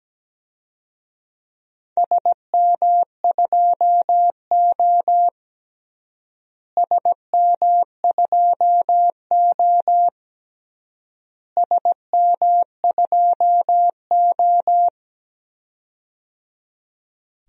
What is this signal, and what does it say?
Morse code, heard as a keyed tone, about 17 wpm, 700 Hz. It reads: SM2O SM2O SM2O
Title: DXCC-100 Call Signs - 4 Characters - Repeated 3x - 17wpm